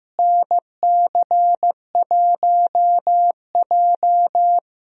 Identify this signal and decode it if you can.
Morse code, heard as a keyed tone, about 15 wpm, 700 Hz. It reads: NC1J
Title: USA Call Signs - 4 Characters - Repeated 3x - 15wpm